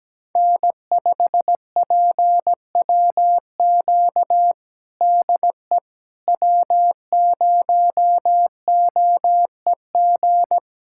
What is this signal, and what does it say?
Morse code, heard as a keyed tone, about 17 wpm, 700 Hz. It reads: N5PWQ DE W0OEG
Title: USA Callsign DE Callsign - 5 Characters - 17wpm